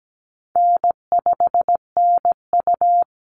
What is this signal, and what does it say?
Morse code, heard as a keyed tone, about 17 wpm, 700 Hz. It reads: N5NU
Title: Call Signs - 17wpm